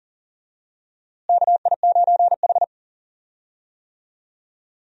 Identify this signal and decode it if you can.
Morse code, heard as a keyed tone, about 40 wpm, 700 Hz. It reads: KI9H
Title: USA Call Signs - 4 Characters - Repeated 2x - 40wpm